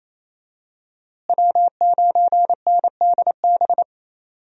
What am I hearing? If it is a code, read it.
Morse code, heard as a keyed tone, about 28 wpm, 700 Hz. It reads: W9NDB
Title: USA Call Signs - 5 Characters - Repeated 3x - 28wpm